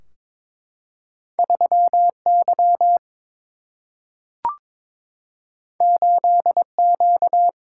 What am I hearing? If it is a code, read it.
Morse code, heard as a keyed tone, about 21 wpm, 700 Hz. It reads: 3 Y 8 Q
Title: Single-Letter Single-Number 22wpm